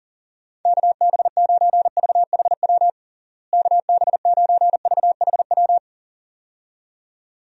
Morse code, heard as a keyed tone, about 40 wpm, 700 Hz. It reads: KB9VHW KB9VHW
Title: USA Call Signs - 6 Characters - Repeated 2x - 40wpm